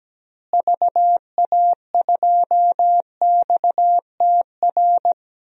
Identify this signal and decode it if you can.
Morse code, heard as a keyed tone, about 17 wpm, 700 Hz. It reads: VA2XTR
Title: DXCC-100 Call Signs - 6 Characters - Repeated 3x - 17wpm